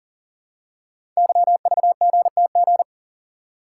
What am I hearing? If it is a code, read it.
Morse code, heard as a keyed tone, about 40 wpm, 700 Hz. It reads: YVGTG
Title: Code Groups - 5 Letters - 40wpm